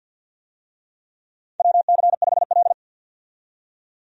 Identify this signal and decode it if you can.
Morse code, heard as a keyed tone, about 52 wpm, 700 Hz. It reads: WC5L
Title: USA Call Signs - 4 Characters - 50wpm